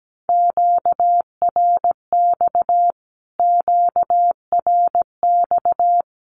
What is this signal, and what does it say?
Morse code, heard as a keyed tone, about 17 wpm, 700 Hz. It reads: QRX QRX
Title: Q-Codes Repeated 3x 17wpm